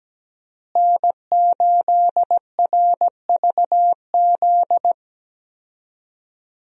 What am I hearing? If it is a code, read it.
Morse code, heard as a keyed tone, about 17 wpm, 700 Hz. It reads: N8RVZ
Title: USA Call Signs - 5 Characters - Repeated 2x - 17wpm